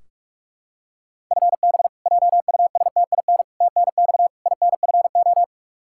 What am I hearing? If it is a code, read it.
Morse code, heard as a keyed tone, about 46 wpm, 700 Hz. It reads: FB JUSTIN TNX INFO
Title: Practice QSO - Exchange 3 - More Info - 45wpm